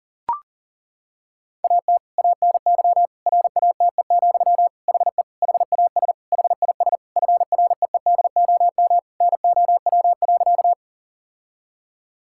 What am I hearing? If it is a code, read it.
Morse code, heard as a keyed tone, about 37 wpm, 700 Hz. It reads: AT ANY RATE, HE HAS HIS FREEDOM NOW.
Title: Sentences from Top 1000 Words - Review Part 5 - 40wpm